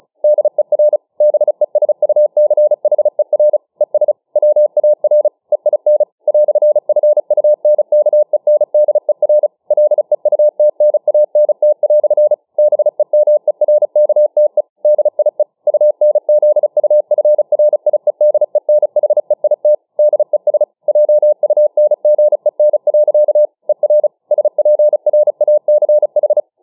Morse code, heard as a keyed tone, about 35 wpm, 600 Hz. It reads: DER BESUCHER ES WAR EIN "FUNKENDER LEUTNANT" BEMERKTE DIE UNZUFRIEDENHEIT DES JUNGEN. ER SPRACH